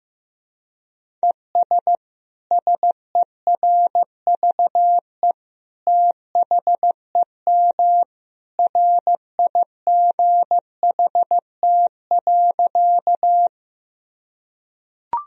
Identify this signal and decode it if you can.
Morse code, heard as a keyed tone, about 16 wpm, 700 Hz. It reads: ES SERVE THEM RIGHT.
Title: Sentences from Top 400 Words and Abbreviations 15wpm